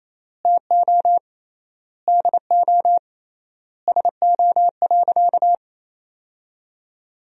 Morse code, heard as a keyed tone, about 28 wpm, 700 Hz. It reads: TO DO SO.
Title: 3x Word-Spacing - Sentences from Top 500 Words - 28wpm